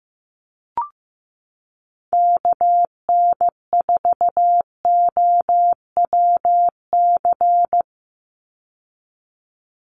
Morse code, heard as a keyed tone, about 15 wpm, 700 Hz. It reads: KN4OWC
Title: USA Call Signs 15wpm